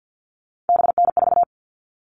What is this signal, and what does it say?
Morse code, heard as a keyed tone, about 48 wpm, 700 Hz. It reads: BN4
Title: All Cut Numbers - 3-Digit 50wpm